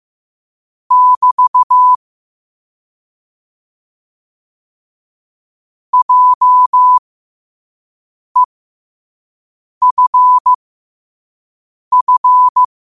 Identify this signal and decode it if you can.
Morse code, heard as a keyed tone, about 15 wpm, spaced as for 5 wpm, 1000 Hz. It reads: <BT> JEFF
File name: qotdfiles / 5wpm